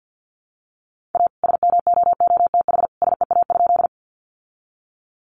Morse code, heard as a keyed tone, about 48 wpm, 700 Hz. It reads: A SMOOTH SEA?
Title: Sentences from Top 1400 Words 50wpm